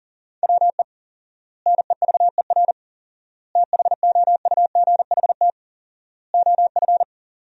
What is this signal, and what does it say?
Morse code, heard as a keyed tone, about 40 wpm, 700 Hz. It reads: WE NEVER THOUGHT OF
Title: 4x Word-Spacing - Sentences from Top 300 Words - 40wpm